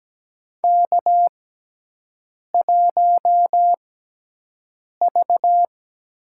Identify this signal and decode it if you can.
Morse code, heard as a keyed tone, about 17 wpm, spaced as for 6 wpm, 700 Hz. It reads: K1V
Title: USA Call Signs - 6x Character Spacing - 17wpm